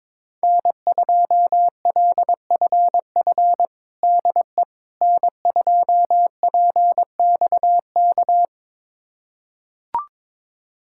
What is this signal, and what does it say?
Morse code, heard as a keyed tone, about 22 wpm, 700 Hz. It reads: N2LFF DE N2PXK
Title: USA Callsign DE Callsign - 5 Characters - 22wpm